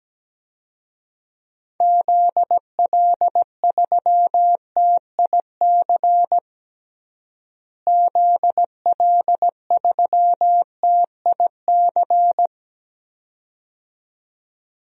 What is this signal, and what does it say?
Morse code, heard as a keyed tone, about 17 wpm, 700 Hz. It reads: ZL3TIC ZL3TIC
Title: DXCC-100 Call Signs - 6 Characters - Repeated 2x - 17wpm